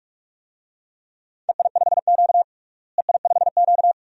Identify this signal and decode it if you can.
Morse code, heard as a keyed tone, about 41 wpm, 700 Hz. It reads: EI5Q EI5Q
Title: DXCC-100 Call Signs - 4 Characters - Repeated 3x - 45wpm